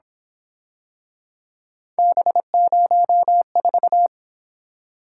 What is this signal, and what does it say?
Morse code, heard as a keyed tone, about 26 wpm, 700 Hz. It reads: B04
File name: Speed-Racing - Three Letter-Number Combinations - 22wpm